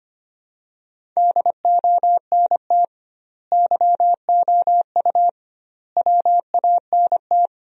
Morse code, heard as a keyed tone, about 25 wpm, 700 Hz. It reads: DONT YOU WANT
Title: 2x Word-Spacing - Sentences from Top 300 Words - 25wpm